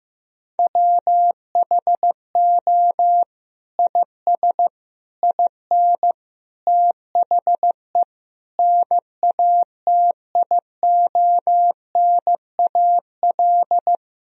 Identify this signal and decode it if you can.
Morse code, heard as a keyed tone, about 15 wpm, 700 Hz. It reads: WHO IS IN THE NATIONAL
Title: Sentences from Top 1000 Words - Review Part 4 - 15wpm